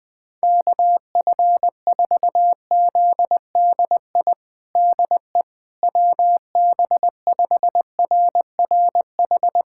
Morse code, heard as a keyed tone, about 21 wpm, 700 Hz. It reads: KF4ZDI DE WB5RRH
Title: USA Callsign DE Callsign - 6 Characters - 20wpm